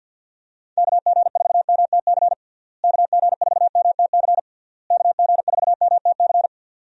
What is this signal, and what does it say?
Morse code, heard as a keyed tone, about 50 wpm, 700 Hz. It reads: KG4MTC KG4MTC KG4MTC
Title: USA Call Signs - 6 Characters - Repeated 3x - 50wpm